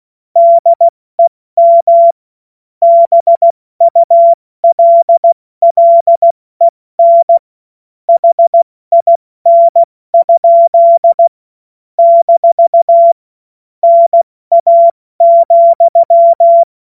Morse code, heard as a keyed tone, about 16 wpm, 680 Hz. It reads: DEM BULLEN HIN? - NA,
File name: Witz-29-Juli-2025-70-F80